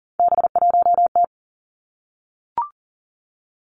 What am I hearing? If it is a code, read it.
Morse code, heard as a keyed tone, about 40 wpm, 700 Hz. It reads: B1T
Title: Three Challenging Letter-Number Combinations 40wpm